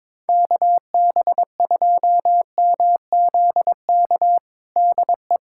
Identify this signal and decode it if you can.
Morse code, heard as a keyed tone, about 22 wpm, 700 Hz. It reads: KB2MZK DE
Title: USA Callsign DE Callsign 22wpm